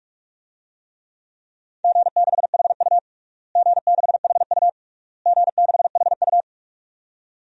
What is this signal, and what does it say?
Morse code, heard as a keyed tone, about 47 wpm, 690 Hz. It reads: G6HU G6HU G6HU
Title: DXCC-100 Call Signs - 4 Characters - Repeated 3x - 45wpm